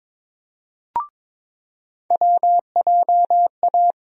Morse code, heard as a keyed tone, about 22 wpm, 700 Hz. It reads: WJA
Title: USA Call Sign Postfixes - 3 Letters 22wpm